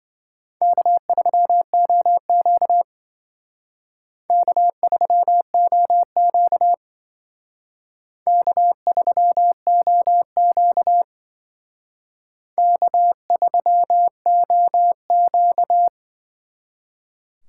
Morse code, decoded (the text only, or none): K3OQ K3OQ K3OQ K3OQ